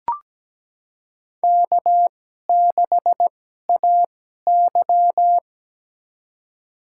Morse code, heard as a keyed tone, about 17 wpm, spaced as for 12 wpm, 700 Hz. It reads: K6AY
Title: USA Call Signs - 4 Characters - 2x Character Spacing - 17wpm